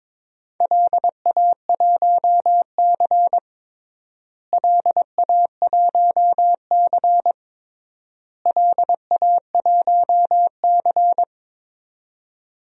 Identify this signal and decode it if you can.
Morse code, heard as a keyed tone, about 22 wpm, 700 Hz. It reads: LA1C LA1C LA1C
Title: DXCC-100 Call Signs - 4 Characters - Repeated 3x - 22wpm